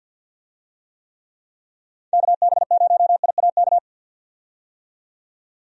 Morse code, heard as a keyed tone, about 53 wpm, 700 Hz. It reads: KB0IAK
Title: USA Call Signs - 6 Characters - 50wpm